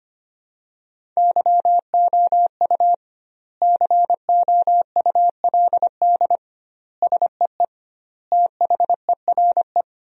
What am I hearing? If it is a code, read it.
Morse code, heard as a keyed tone, about 25 wpm, 700 Hz. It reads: YOU COULD SEE THERE
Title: 2x Word-Spacing - Sentences from Top 500 Words - 25wpm